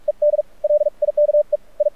Morse code, heard as a keyed tone, about 44 wpm, 590 Hz. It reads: ED LIKE I
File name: archived / CRSnet